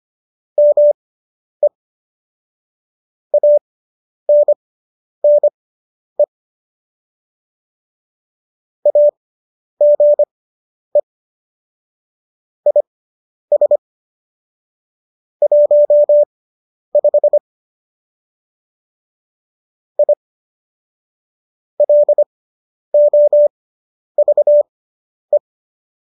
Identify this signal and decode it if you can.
Morse code, heard as a keyed tone, about 25 wpm, spaced as for 10 wpm, 590 Hz. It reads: ME ANNE AGE IS 15 I LOVE